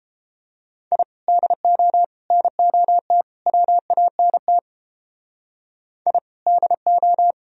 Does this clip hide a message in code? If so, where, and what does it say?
Yes: Morse code from 0.9 to 7.3 s, heard as a keyed tone, about 31 wpm, 700 Hz. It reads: I DO NOT WANT I DO